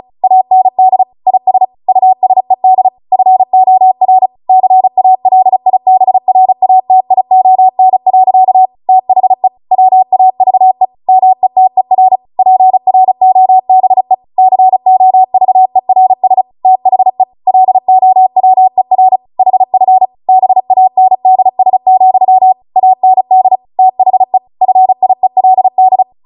Morse code, heard as a keyed tone, about 36 wpm, 750 Hz. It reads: AND IS USED FOR CALIBRATION. THE WAVE METER PROBE COVERS THE LOWER HF BANDS, AND THE FIELD